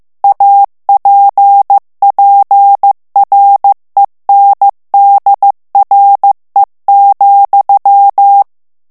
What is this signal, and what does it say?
Morse code, heard as a keyed tone, about 15 wpm, 800 Hz. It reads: APPRENDRE,
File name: FRANCAIS TEXTE CLAIR 2 VITESSE 14 MOTS MINUTES